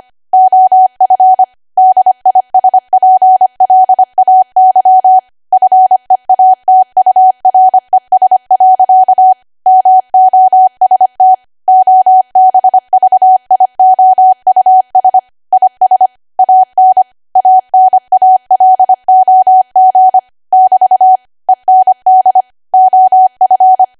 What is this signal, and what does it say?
Morse code, heard as a keyed tone, about 25 wpm, 750 Hz. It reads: OF DISPLAY FEATURES. MOST OBVIOUS IS AN ANALOG <BT> END OF
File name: cwa / W1AW-Level3 / 25WPM